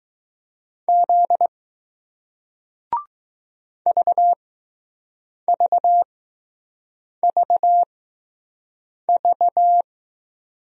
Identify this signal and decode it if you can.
Morse code, heard as a keyed tone, about 20 wpm, spaced as for 6 wpm, 700 Hz. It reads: Z VVVV